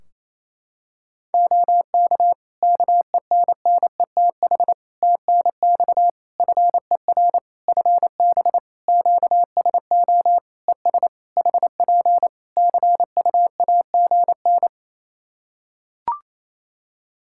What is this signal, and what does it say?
Morse code, heard as a keyed tone, about 27 wpm, 700 Hz. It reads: OK KENNETH TNX FER FB QSO ES HP CUAGN
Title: Practice QSO - Exchange 4 - Ending - 28wpm